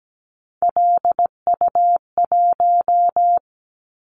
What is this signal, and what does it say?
Morse code, heard as a keyed tone, about 17 wpm, 700 Hz. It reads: LU1
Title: Callsign Prefixes for Top 50 DXCC - 17wpm